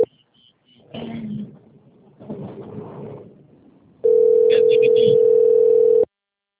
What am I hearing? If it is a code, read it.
Ringback tone (North America), 440 + 480 Hz.
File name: rg-1600-unknown- - - .61214